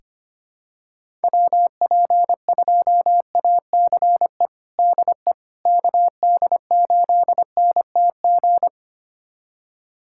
Morse code, heard as a keyed tone, about 25 wpm, 700 Hz. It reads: WP2ACE DE KD8NTG
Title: USA Callsign DE Callsign - 6 Characters - 25wpm